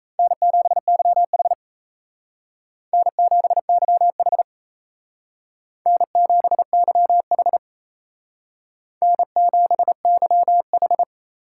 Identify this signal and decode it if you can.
Morse code, heard as a keyed tone, about 34 wpm, 700 Hz. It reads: N7YH N7YH N7YH N7YH
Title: Speed Racing - USA Call Signs 28wpm